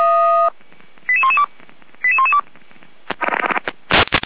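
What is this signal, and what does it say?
Five-tone selective call (ZVEI), heard as quick runs of five tones, tones of about 70 ms. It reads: 88112 88112
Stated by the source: ZVEI five-tone selective call 88491 received by SDR.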